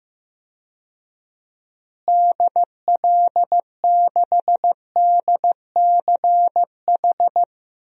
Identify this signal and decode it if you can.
Morse code, heard as a keyed tone, about 16 wpm, 700 Hz. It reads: DL6DCH
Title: DXCC-100 Call Signs - 6 Characters - Repeated 3x - 15wpm